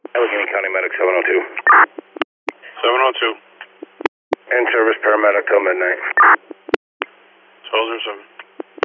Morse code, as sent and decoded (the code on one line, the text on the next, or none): none